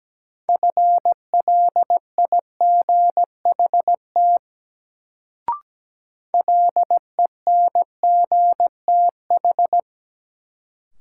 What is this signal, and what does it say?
Morse code, heard as a keyed tone, about 17 wpm, 700 Hz. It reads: FLIGHT LENGTH
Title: Sentences from Top 1000 Words - New Word Drill - 17wpm